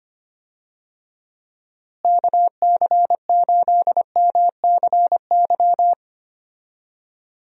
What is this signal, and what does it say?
Morse code, heard as a keyed tone, about 25 wpm, 700 Hz. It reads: KC8MCY
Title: USA Call Signs - 6 Characters - Repeated 3x - 25wpm